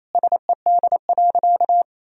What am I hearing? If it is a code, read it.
Morse code, heard as a keyed tone, about 28 wpm, 700 Hz. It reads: SED.
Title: Sentences from Top 400 Words and Abbreviations 28wpm